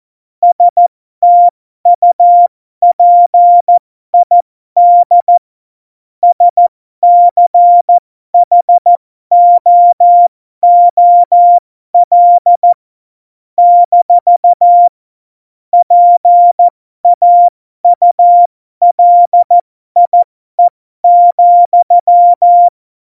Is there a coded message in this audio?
Yes: Morse code from 0.4 to 22.7 s, heard as a keyed tone, about 14 wpm, spaced as for 12 wpm, 700 Hz. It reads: STUPID SCHOOL - PAULIE,